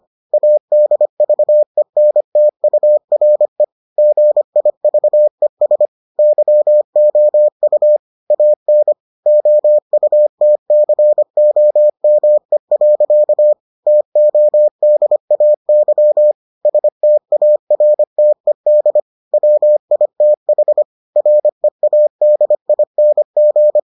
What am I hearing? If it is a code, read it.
Morse code, heard as a keyed tone, about 25 wpm, 600 Hz. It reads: ADVENTURE GIVES YOU AN OUTCOME. TODAY STARTED WITH READING